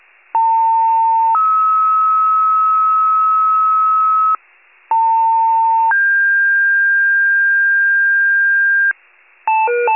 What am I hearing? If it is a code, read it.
Two-tone page, heard as a one-second tone, then a three-second tone, about 900 then 1290 Hz, about 900 then 1600 Hz, with noise. It is heard from 0.3 to 8.9 s.